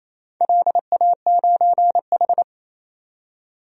Morse code, heard as a keyed tone, about 28 wpm, 700 Hz. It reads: LA9H
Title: DXCC-100 Call Signs - 4 Characters - 28wpm